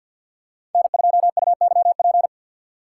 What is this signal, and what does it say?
Morse code, heard as a keyed tone, about 52 wpm, 690 Hz. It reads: N2UYP
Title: USA Call Signs - 3x Character Spacing - 50wpm